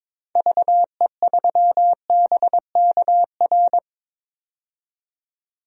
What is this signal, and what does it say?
Morse code, heard as a keyed tone, about 22 wpm, 700 Hz. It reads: VE3BKR
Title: DXCC-100 Call Signs - 6 Characters - 22wpm